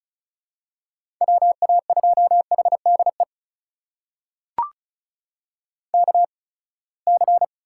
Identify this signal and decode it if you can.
Morse code, heard as a keyed tone, about 34 wpm, spaced as for 24 wpm, 700 Hz. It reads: WA2HDE KC
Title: USA Call Signs - 6 Characters - 8x Character Spacing - 35wpm